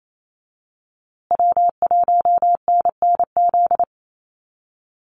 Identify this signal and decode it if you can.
Morse code, heard as a keyed tone, about 28 wpm, 700 Hz. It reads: W1NNZ
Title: USA Call Signs Repeated 3x 28wpm